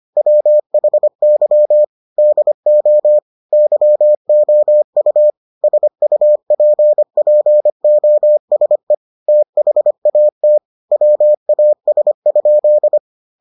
Morse code, heard as a keyed tone, about 25 wpm, 600 Hz. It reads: WHY DO YOU SUPPOSE THAT WAS?